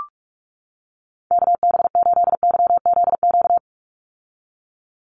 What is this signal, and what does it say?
Morse code, heard as a keyed tone, about 44 wpm, 700 Hz. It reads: KB8YZQ
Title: USA Call Signs 45wpm